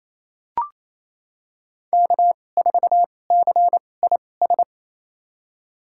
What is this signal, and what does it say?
Morse code, heard as a keyed tone, about 28 wpm, spaced as for 20 wpm, 700 Hz. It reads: K4CIS
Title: USA Call Signs - 5 Characters - 2x Character Spacing - 28wpm